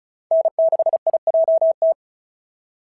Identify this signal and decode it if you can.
Morse code, heard as a keyed tone, about 36 wpm, 650 Hz. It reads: N6IJT
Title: SOTA QSO Practice 35wpm